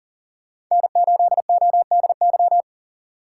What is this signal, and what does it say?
Morse code, heard as a keyed tone, about 39 wpm, 700 Hz. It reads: N8ODY
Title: USA Call Signs - 5 Characters - 2x Character Spacing - 40wpm